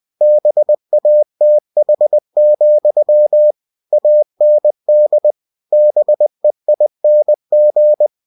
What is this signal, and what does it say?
Morse code, heard as a keyed tone, about 20 wpm, 600 Hz. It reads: BATH, AND BEING
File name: cw-books / winnie-the-pooh / winnie-20wpm-page-0072